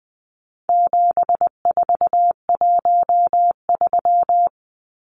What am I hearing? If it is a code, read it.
Morse code, heard as a keyed tone, about 20 wpm, 700 Hz. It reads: 7413
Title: Numbers - 4-Digit 20wpm